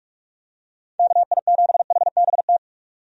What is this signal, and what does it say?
Morse code, heard as a keyed tone, about 46 wpm, 700 Hz. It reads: KI7HBT
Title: USA Call Signs - 6 Characters - 45wpm